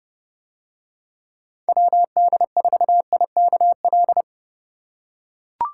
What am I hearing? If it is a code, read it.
Morse code, heard as a keyed tone, about 31 wpm, 700 Hz. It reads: WD4IKL